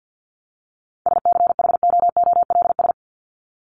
Morse code, heard as a keyed tone, about 50 wpm, 700 Hz. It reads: SCHOOLS
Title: ES:S Suffix - Top 100 Words with Base Words 50wpm